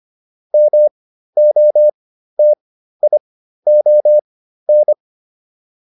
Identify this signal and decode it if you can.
Morse code, heard as a keyed tone, about 25 wpm, spaced as for 13 wpm, 600 Hz. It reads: MOTION